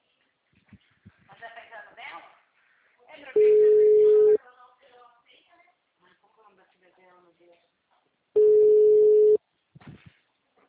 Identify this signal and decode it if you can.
Ringback tone (Europe, 425 Hz), with speech.